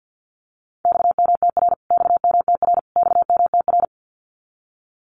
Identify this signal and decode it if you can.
Morse code, heard as a keyed tone, about 50 wpm, 700 Hz. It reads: XMTR XMTR XMTR
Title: Abbreviations Repeated 3x - 50wpm